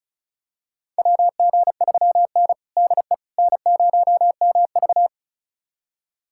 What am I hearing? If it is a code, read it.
Morse code, heard as a keyed tone, about 36 wpm, 700 Hz. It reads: WG3N DE N0MV
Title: USA Callsign DE Callsign - 4 Characters - 35wpm